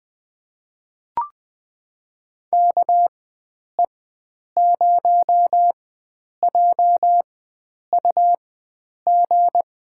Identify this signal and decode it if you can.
Morse code, heard as a keyed tone, about 20 wpm, spaced as for 9 wpm, 700 Hz. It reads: KE0JUG
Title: USA Call Signs - 6 Characters - 4x Character Spacing - 20wpm